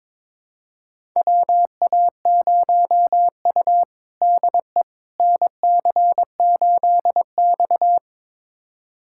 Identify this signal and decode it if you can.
Morse code, heard as a keyed tone, about 21 wpm, 700 Hz. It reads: WA0U DE NC8X
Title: USA Callsign DE Callsign - 4 Characters - 22wpm